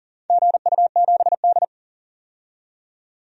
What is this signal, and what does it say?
Morse code, heard as a keyed tone, about 40 wpm, 700 Hz. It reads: GU7D
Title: DXCC-100 Call Signs - 4 Characters - 40wpm